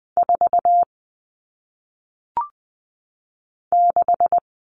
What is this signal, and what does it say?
Morse code, heard as a keyed tone, about 20 wpm, 700 Hz. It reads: 4 6
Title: All Cut Numbers - 1-Digit 20wpm